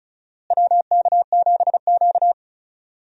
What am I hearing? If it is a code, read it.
Morse code, heard as a keyed tone, about 36 wpm, 700 Hz. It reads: WK7Q
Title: USA Call Signs - 4 Characters - Repeated 3x - 35wpm